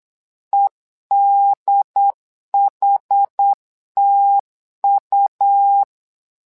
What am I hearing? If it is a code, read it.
Morse code, heard as a keyed tone, about 8 wpm, 800 Hz. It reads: EDHTU
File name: LEZIONI / 2 UHDN / 2 UHD 8wpm 2258